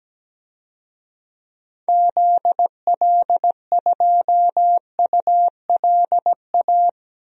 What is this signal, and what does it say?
Morse code, heard as a keyed tone, about 17 wpm, 700 Hz. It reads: ZL2ULA
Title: DXCC-100 Call Signs - 6 Characters - Repeated 2x - 17wpm